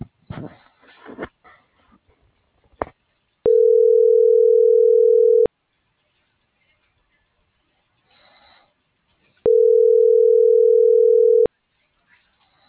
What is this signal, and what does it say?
Ringback tone (North America), 440 + 480 Hz.